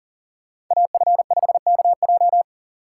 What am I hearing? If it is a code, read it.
Morse code, heard as a keyed tone, about 40 wpm, 700 Hz. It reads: AF5KJ
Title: USA Call Signs - 5 Characters - 6x Character Spacing - 40wpm